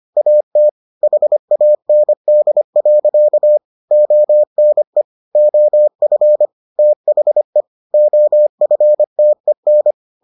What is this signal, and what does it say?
Morse code, heard as a keyed tone, about 25 wpm, 600 Hz. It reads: AT HAND. ONE OF THE OFTEN